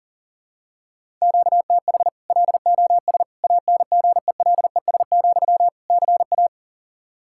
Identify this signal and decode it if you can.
Morse code, heard as a keyed tone, about 40 wpm, 700 Hz. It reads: QTH LOS ANGELES, CA
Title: Practice QSO - Exchange 1 - Beginning - 40wpm